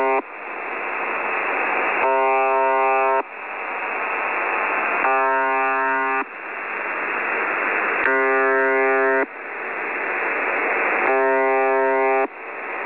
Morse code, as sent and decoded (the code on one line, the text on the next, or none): none